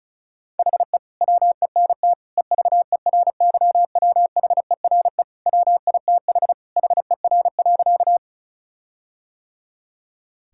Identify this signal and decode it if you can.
Morse code, heard as a keyed tone, about 34 wpm, 700 Hz. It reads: HE WENT EVERYWHERE WITH HER.
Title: Sentences from Top 1600 Words 35wpm